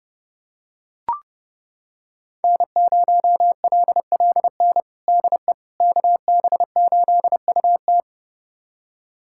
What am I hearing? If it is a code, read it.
Morse code, heard as a keyed tone, about 30 wpm, 700 Hz. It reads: N0LLN DE KB8UT
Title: USA Callsign DE Callsign - 5 Characters - 30wpm